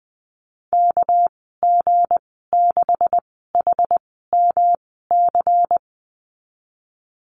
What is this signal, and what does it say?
Morse code, heard as a keyed tone, about 19 wpm, spaced as for 14 wpm, 700 Hz. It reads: KG6HMC
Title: 207 Lesson 33g - Letter j - Call Signs 20wpm